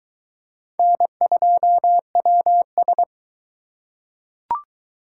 Morse code, heard as a keyed tone, about 21 wpm, 700 Hz. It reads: N2WS